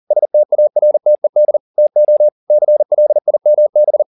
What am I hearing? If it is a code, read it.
Morse code, heard as a keyed tone, about 40 wpm, 600 Hz. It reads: STARTED TO CLIMB
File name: the old man and the sea 2 40wpm